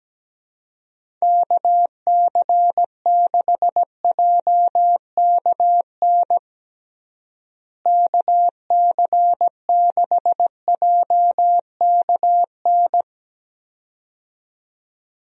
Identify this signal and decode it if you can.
Morse code, heard as a keyed tone, about 17 wpm, 700 Hz. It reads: KC6JKN KC6JKN
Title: USA Call Signs - 6 Characters - Repeated 2x - 17wpm